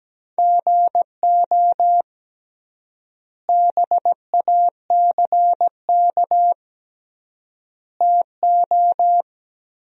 Morse code, heard as a keyed tone, about 17 wpm, 700 Hz. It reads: GO BACK TO
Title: 3x Word-Spacing - Sentences from Top 500 Words - 17wpm